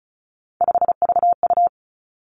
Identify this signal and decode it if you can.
Morse code, heard as a keyed tone, about 36 wpm, 700 Hz. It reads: 5VU
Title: Three Challenging Letter-Number Combinations 35wpm